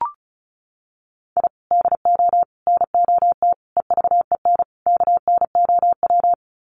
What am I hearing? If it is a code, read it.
Morse code, heard as a keyed tone, about 34 wpm, 700 Hz. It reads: I DO NOT EVEN KNOW
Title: Sets of 5 Words 35wpm